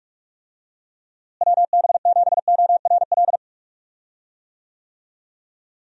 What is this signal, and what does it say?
Morse code, heard as a keyed tone, about 45 wpm, 700 Hz. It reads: WB7ORL
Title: USA Call Signs - 6 Characters - Repeated 3x - 45wpm